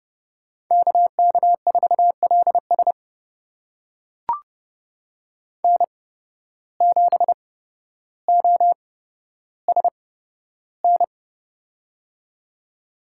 Morse code, heard as a keyed tone, about 30 wpm, spaced as for 13 wpm, 700 Hz. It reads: KK4LS N7OSN
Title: USA Call Signs - 5 Characters - 8x Character Spacing - 30wpm